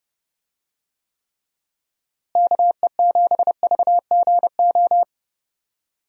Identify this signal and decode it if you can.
Morse code, heard as a keyed tone, about 30 wpm, 700 Hz. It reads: KE7VGO
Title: USA Call Signs - 6 Characters - Repeated 3x - 30wpm